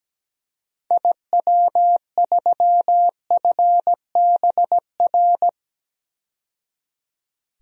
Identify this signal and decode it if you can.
Morse code, heard as a keyed tone, about 17 wpm, 700 Hz. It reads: IW3FBR
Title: DXCC-100 Call Signs - 6 Characters - 17wpm